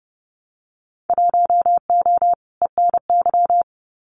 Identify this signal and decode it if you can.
Morse code, heard as a keyed tone, about 30 wpm, 700 Hz. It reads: 1O ENY
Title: Winter Field Day Exchanges 30wpm